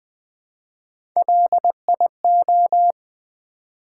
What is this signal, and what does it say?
Morse code, heard as a keyed tone, about 20 wpm, 700 Hz. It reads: LIO
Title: Build A Word - 1 Letter At A Time - 3 to 5-Letter Words - 20wpm